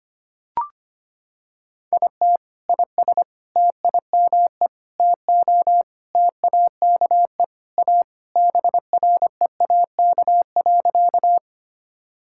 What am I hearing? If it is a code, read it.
Morse code, heard as a keyed tone, about 25 wpm, 700 Hz. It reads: IT IS TIME TO TAKE A BREAK.
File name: practice / 25wpm / Sentences with 4-Word Phrases - 25wpm